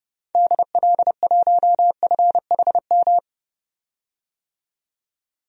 Morse code, heard as a keyed tone, about 32 wpm, 700 Hz. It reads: DL1FHM
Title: DXCC-100 Call Signs - 6 Characters - 30wpm